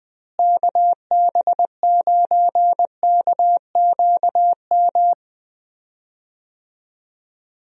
Morse code, heard as a keyed tone, about 20 wpm, 700 Hz. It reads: KB9KQM
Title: USA Call Signs - 6 Characters - Repeated 2x - 20wpm